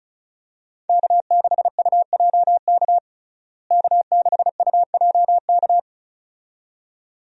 Morse code, heard as a keyed tone, about 34 wpm, 700 Hz. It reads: K6UJK K6UJK
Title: USA Call Signs - 5 Characters - Repeated 2x - 35wpm